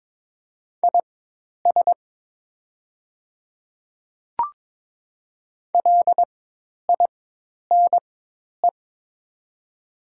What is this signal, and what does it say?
Morse code, heard as a keyed tone, about 22 wpm, spaced as for 10 wpm, 700 Hz. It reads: IS LINE
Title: 4x Character Spacing - Top 300 Words - 22wpm